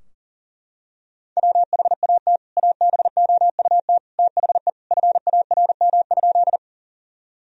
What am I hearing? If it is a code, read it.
Morse code, heard as a keyed tone, about 37 wpm, 700 Hz. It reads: WHAT ABOUT THE FARM?
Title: Sentences from Top 1000 Words - Review Part 2 - 40wpm